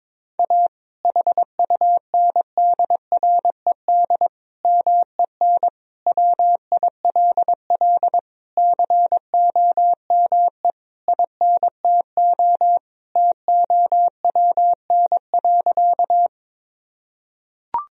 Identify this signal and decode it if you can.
Morse code, heard as a keyed tone, about 22 wpm, 700 Hz. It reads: A HUNDRED MEN WILL COME INTO TOWN.
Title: Sentences from Top 1000 Words - Review Part 2 - 22wpm